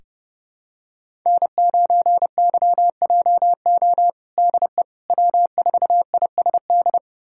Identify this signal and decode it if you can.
Morse code, heard as a keyed tone, about 30 wpm, 700 Hz. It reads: N9YJO DE W4ISD
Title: USA Callsign DE Callsign - 5 Characters - 30wpm